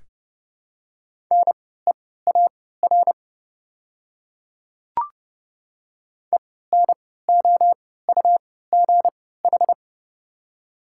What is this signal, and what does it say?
Morse code, heard as a keyed tone, about 31 wpm, spaced as for 17 wpm, 700 Hz. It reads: NEAR ENOUGH
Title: 3x Character Spacing - Top 500 Words - 30wpm